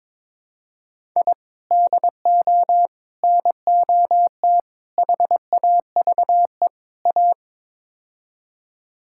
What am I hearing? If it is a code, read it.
Morse code, heard as a keyed tone, about 22 wpm, 700 Hz. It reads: I DO NOT HAVE A
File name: practice / 22wpm / Build A Phrase - 5-Word Phrases - 22wpm